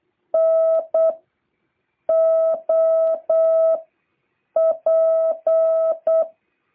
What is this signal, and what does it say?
Morse code, heard as a keyed tone, about 8 wpm, 650 Hz. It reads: N O P